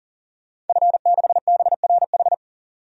Morse code, heard as a keyed tone, about 37 wpm, 700 Hz. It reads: F6BRH
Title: DXCC-100 Call Signs - 5 Characters - 40wpm